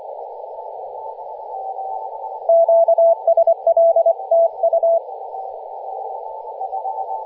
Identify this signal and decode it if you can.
Morse code, heard as a keyed tone, about 24 wpm, 670 Hz, with radio noise. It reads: QSL TU
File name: posts / 2016-nov-10-0528z-t30ma-21-007mhz-cw